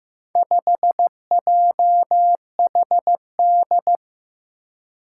Morse code, heard as a keyed tone, about 15 wpm, 700 Hz. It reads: 5JHD
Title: Four Challenging Letter-Number Combinations 15wpm